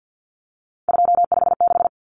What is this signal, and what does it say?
Morse code, heard as a keyed tone, about 50 wpm, 700 Hz. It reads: 256
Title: Common Cut Numbers - 3-Digit 50wpm